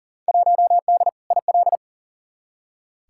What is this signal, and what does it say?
Morse code, heard as a keyed tone, about 40 wpm, 700 Hz. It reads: 1D IL
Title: Field Day Exchanges - 40wpm